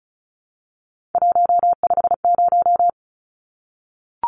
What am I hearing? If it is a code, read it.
Morse code, heard as a keyed tone, about 34 wpm, 700 Hz. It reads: 150
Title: Numbers - 3-Digit 35wpm